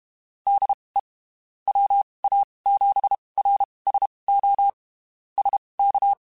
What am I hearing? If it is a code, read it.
Morse code, heard as a keyed tone, about 32 wpm, spaced as for 23 wpm, 800 Hz. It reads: DE WA7RSO SK
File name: (spaced) / QSOs Spaced-mp3s / WA7QOX 2 @30 sp